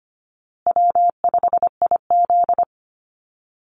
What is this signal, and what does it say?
Morse code, heard as a keyed tone, about 25 wpm, 700 Hz. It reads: W5IZ
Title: Easy Call Signs Repeated 3x - 25wpm